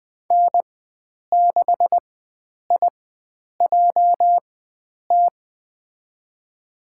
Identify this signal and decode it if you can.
Morse code, heard as a keyed tone, about 20 wpm, spaced as for 9 wpm, 700 Hz. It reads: N6IJT
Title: USA Call Signs - 5 Characters - 4x Character Spacing - 20wpm